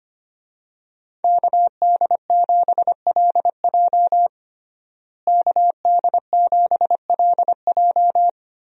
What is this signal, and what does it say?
Morse code, heard as a keyed tone, about 25 wpm, 700 Hz. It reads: KD7LJ KD7LJ
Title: USA Call Signs - 5 Characters - Repeated 3x - 25wpm